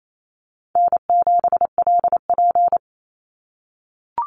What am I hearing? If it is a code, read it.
Morse code, heard as a keyed tone, about 28 wpm, 700 Hz. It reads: N7LP
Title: NA SOTA Chaser Callsigns 28wpm